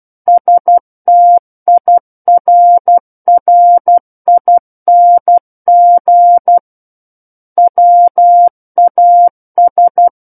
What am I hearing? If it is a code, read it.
Morse code, heard as a keyed tone, about 12 wpm, 700 Hz. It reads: STIRRING WAS
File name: Book One / 03-Chapter 3 - 12 WPM